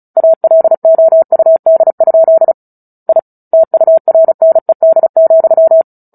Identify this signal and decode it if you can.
Morse code, heard as a keyed tone, about 34 wpm, 650 Hz. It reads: ALOUD? I TURNED,
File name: Book Two / 09-Chapter 9 - 35 WPM